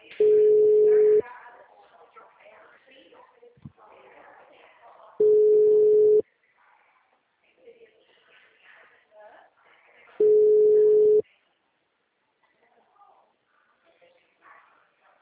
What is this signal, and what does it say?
Ringback tone (Europe, 425 Hz), with speech.
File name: out- -2031- - - .10284